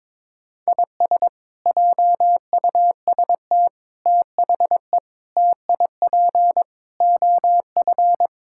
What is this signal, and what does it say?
Morse code, heard as a keyed tone, about 21 wpm, 700 Hz. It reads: IS JUST THE TIP OF
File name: practice / 22wpm / Sets of 7 Words - 22wpm